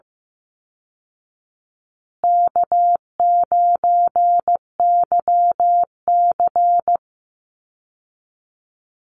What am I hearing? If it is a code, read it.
Morse code, heard as a keyed tone, about 14 wpm, 700 Hz. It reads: K9YC
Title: Easy Call Signs Repeated 3x - 15wpm